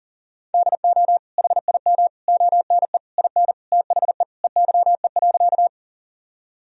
Morse code, heard as a keyed tone, about 40 wpm, 700 Hz. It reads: DO HIM ONE IN THE EYE.
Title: Sentences from Top 1000 Words - Review Part 1 - 40wpm